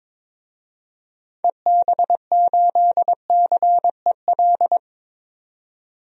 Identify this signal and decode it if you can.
Morse code, heard as a keyed tone, about 21 wpm, 700 Hz. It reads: EB8CEL
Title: DXCC-100 Call Signs - 6 Characters - 22wpm